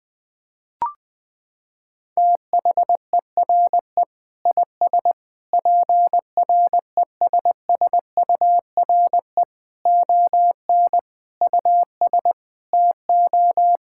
Morse code, heard as a keyed tone, about 20 wpm, 700 Hz. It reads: THERE IS PRESSURE ON US TO
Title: Sentences from Top 1000 Words - Review Part 6 - 20wpm